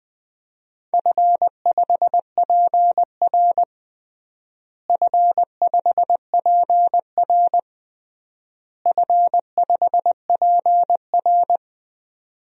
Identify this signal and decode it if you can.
Morse code, heard as a keyed tone, about 20 wpm, 700 Hz. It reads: F5PR F5PR F5PR
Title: DXCC-100 Call Signs - 4 Characters - Repeated 3x - 20wpm